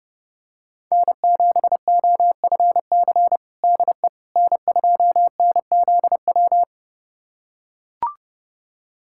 Morse code, heard as a keyed tone, about 30 wpm, 700 Hz. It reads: N7OFC DE N2NZW
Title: USA Callsign DE Callsign - 5 Characters - 30wpm